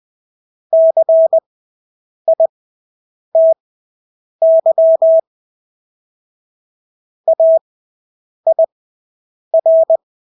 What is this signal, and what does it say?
Morse code, heard as a keyed tone, about 20 wpm, spaced as for 8 wpm, 650 Hz. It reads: CITY AIR
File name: Engelske ord 20/8 wpm